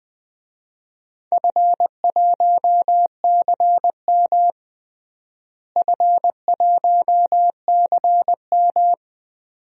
Morse code, heard as a keyed tone, about 20 wpm, 700 Hz. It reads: F1CM F1CM
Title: DXCC-100 Call Signs - 4 Characters - Repeated 2x - 20wpm